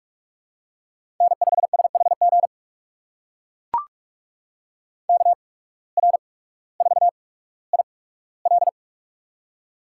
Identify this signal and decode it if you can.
Morse code, heard as a keyed tone, about 44 wpm, spaced as for 19 wpm, 700 Hz. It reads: N5SHG KR4IL
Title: USA Call Signs - 5 Characters - 8x Character Spacing - 45wpm